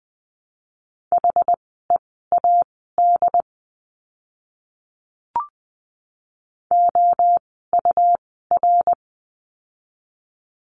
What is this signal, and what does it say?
Morse code, heard as a keyed tone, about 20 wpm, spaced as for 14 wpm, 700 Hz. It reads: HEAD OUR
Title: 115 Lesson 19f - Letter w - Words 20wpm